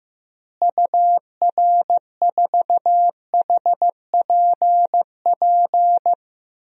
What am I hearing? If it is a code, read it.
Morse code, heard as a keyed tone, about 15 wpm, 700 Hz. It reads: UR4HPP
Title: DXCC-100 Call Signs - 6 Characters - Repeated 3x - 15wpm